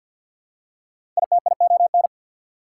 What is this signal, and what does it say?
Morse code, heard as a keyed tone, about 49 wpm, 690 Hz. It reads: ITION
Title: Prefixes and Suffixes 50wpm